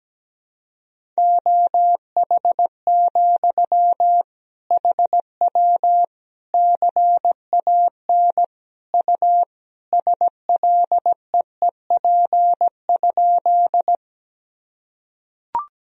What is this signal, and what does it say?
Morse code, heard as a keyed tone, about 17 wpm, 700 Hz. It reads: OH, HW CAN U SLEEP?
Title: Sentences from Top 500 Words and Abbreviations 17wpm